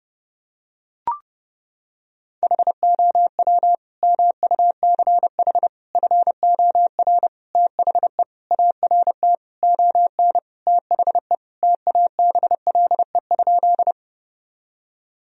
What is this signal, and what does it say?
Morse code, heard as a keyed tone, about 30 wpm, 700 Hz. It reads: HOW MUCH FOR THE ART ON THE TABLE?
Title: Sentences from Top 1000 Words - Review Part 3 - 30wpm